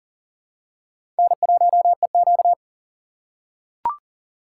Morse code, heard as a keyed tone, about 39 wpm, 700 Hz. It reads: N1EQ
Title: USA Call Signs - 4 Characters - 3x Character Spacing - 40wpm